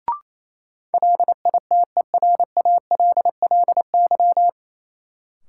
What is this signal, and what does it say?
Morse code, heard as a keyed tone, about 28 wpm, 700 Hz. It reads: LITERALLY
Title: Sentences from Top 1400 Words - New Word Drill - 28wpm